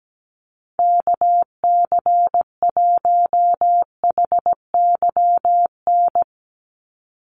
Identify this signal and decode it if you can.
Morse code, heard as a keyed tone, about 17 wpm, 700 Hz. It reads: KC1HYN
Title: USA Call Signs 17wpm